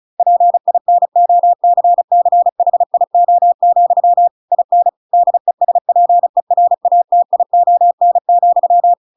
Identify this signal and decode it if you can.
Morse code, heard as a keyed tone, about 34 wpm, 700 Hz. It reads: PINOCCHIO, IN DESPERATION,